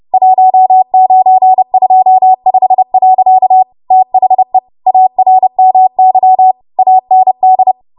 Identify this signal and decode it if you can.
Morse code, heard as a keyed tone, about 30 wpm, 750 Hz. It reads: 1925. THE ARMY AND
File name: Archive / 30 WPM / 30WPM